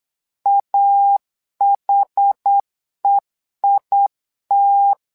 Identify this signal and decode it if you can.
Morse code, heard as a keyed tone, about 8 wpm, 800 Hz. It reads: AHEIT
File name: LEZIONI / 2 UHDN / 2 UH 8wpm 2129